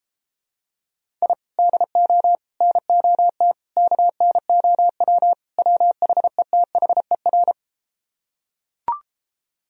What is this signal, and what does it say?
Morse code, heard as a keyed tone, about 33 wpm, 700 Hz. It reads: I DO NOT KNOW WHETHER